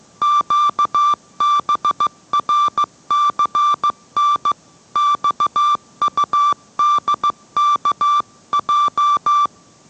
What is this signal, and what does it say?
Morse code, heard as a keyed tone, about 15 wpm, 1170 Hz. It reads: QBRCN XUDKJ